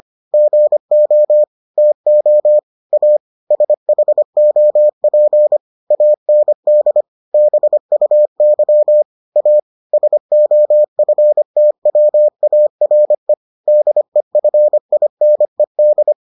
Morse code, heard as a keyed tone, about 25 wpm, 600 Hz. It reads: GO TO A SHOP AND BUY A SOFTWARE DEFINED